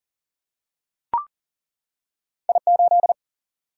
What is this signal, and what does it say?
Morse code, heard as a keyed tone, about 40 wpm, 700 Hz. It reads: I8
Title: Single-Letter Single-Number 40wpm